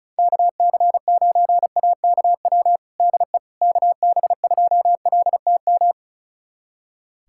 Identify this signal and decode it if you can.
Morse code, heard as a keyed tone, about 36 wpm, 700 Hz. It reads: KC9AKW DE KB2LTM
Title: USA Callsign DE Callsign 35wpm